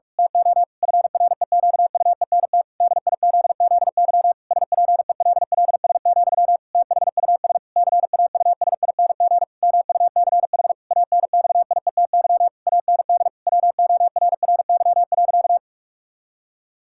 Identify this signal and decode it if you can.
Morse code, heard as a keyed tone, about 46 wpm, 700 Hz. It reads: TO FREQUENT DIZZY SPELLS, THUS CAUSING MUCH ANXIETY AND WORRY.
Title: Pangram Challenge v2 - 45wpm